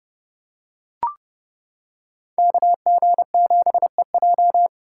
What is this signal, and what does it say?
Morse code, heard as a keyed tone, about 29 wpm, 700 Hz. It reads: KG7EJ